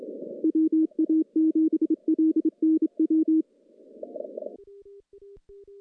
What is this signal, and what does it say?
Morse code, heard as a keyed tone, about 27 wpm, 320 Hz. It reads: WA7LNW
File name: contests / CW-SS-k5tr- -1300